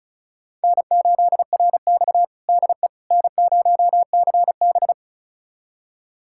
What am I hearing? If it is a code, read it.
Morse code, heard as a keyed tone, about 36 wpm, 700 Hz. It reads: N8RX DE N0CB
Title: USA Callsign DE Callsign - 4 Characters - 35wpm